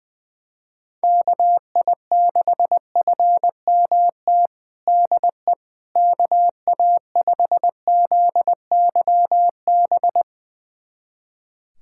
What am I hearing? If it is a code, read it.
Morse code, heard as a keyed tone, about 20 wpm, 700 Hz. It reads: KI6FMT DE KA5ZYB
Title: USA Callsign DE Callsign - 6 Characters - 20wpm